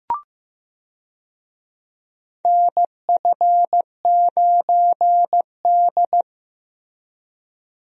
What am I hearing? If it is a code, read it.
Morse code, heard as a keyed tone, about 15 wpm, 700 Hz. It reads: NF9D
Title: USA Call Signs - 4 Characters - Repeated 2x - 15wpm